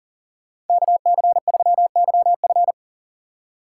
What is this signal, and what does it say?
Morse code, heard as a keyed tone, about 40 wpm, 700 Hz. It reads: KC3YF
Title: USA Call Signs - 5 Characters - 2x Character Spacing - 40wpm